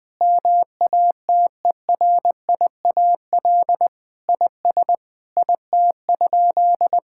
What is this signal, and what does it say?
Morse code, heard as a keyed tone, about 21 wpm, 700 Hz. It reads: MATERIAL IS IT?
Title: Sentences from Top 1000 Words - Review Part 5 - 20wpm